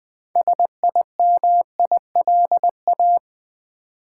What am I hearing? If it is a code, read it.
Morse code, heard as a keyed tone, about 20 wpm, 700 Hz. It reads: SIMILA
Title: Build A Word - 3 Letters At A Time - 9-Letter Words - 20wpm